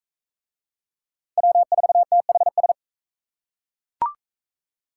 Morse code, heard as a keyed tone, about 41 wpm, 700 Hz. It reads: W4THS